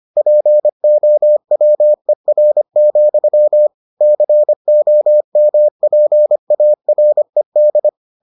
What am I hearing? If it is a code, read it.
Morse code, heard as a keyed tone, about 25 wpm, 600 Hz. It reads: POWER, COMPARED